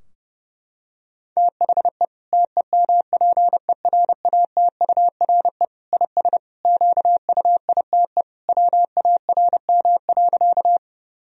Morse code, heard as a keyed tone, about 30 wpm, 700 Hz. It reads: THE TEMPERATURE IS QUITE WARM.
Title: Sentences from Top 1000 Words - Review Part 4 - 30wpm